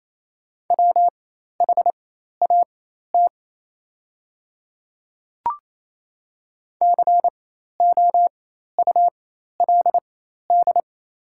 Morse code, heard as a keyed tone, about 28 wpm, spaced as for 13 wpm, 690 Hz. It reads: WHAT COULD
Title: 4x Character Spacing - Top 100 Words - 28wpm